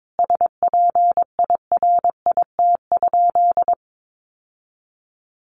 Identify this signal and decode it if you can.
Morse code, heard as a keyed tone, about 22 wpm, 700 Hz. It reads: SPIRIT?
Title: Sentences from Top 1400 Words 22wpm